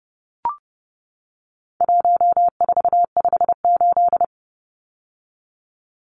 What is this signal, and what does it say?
Morse code, heard as a keyed tone, about 30 wpm, 700 Hz. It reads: 1458
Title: Numbers - 4-Digit 30wpm